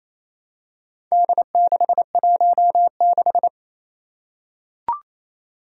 Morse code, heard as a keyed tone, about 27 wpm, 700 Hz. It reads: D616
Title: Four Challenging Letter-Number Combinations 28wpm